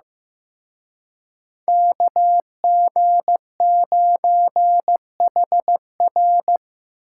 Morse code, heard as a keyed tone, about 18 wpm, 700 Hz. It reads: KG9HR
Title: Speed Racing - USA 2x2 Call Signs 15wpm